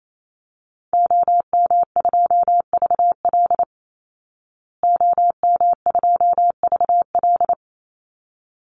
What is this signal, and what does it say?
Morse code, heard as a keyed tone, about 28 wpm, 700 Hz. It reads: OM2VL OM2VL
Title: Call Signs Repeated 3x - 28wpm